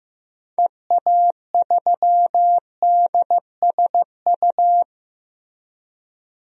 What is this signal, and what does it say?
Morse code, heard as a keyed tone, about 16 wpm, 700 Hz. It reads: EA3DSU
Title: DXCC-100 Call Signs - 6 Characters - 15wpm